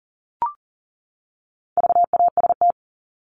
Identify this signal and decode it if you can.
Morse code, heard as a keyed tone, about 40 wpm, 700 Hz. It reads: VAST
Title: Top 1251-1500 Words 40wpm